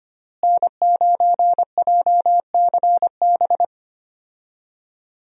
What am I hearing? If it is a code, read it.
Morse code, heard as a keyed tone, about 25 wpm, 700 Hz. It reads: N9JCB
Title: USA Call Signs - 5 Characters - Repeated 3x - 25wpm